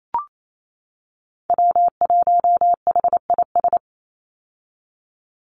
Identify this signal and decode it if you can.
Morse code, heard as a keyed tone, about 28 wpm, 700 Hz. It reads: W1HIS
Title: Call Signs - 28wpm